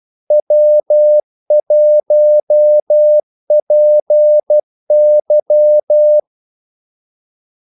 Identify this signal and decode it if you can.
Morse code, heard as a keyed tone, about 12 wpm, 600 Hz. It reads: W1PY
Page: 2 ekstra ordmellemrum mellem kaldesignalerne